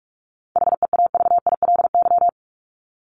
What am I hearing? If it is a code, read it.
Morse code, heard as a keyed tone, about 44 wpm, 700 Hz. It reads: HEAVILY
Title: LY Suffix - Top 100 Words with Base Words 45wpm